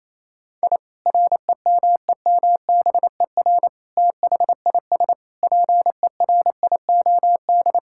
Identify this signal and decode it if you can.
Morse code, heard as a keyed tone, about 28 wpm, 700 Hz. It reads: I REMEMBER THIS PERIOD
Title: Sentences from Top 1000 Words - Review Part 5 - 28wpm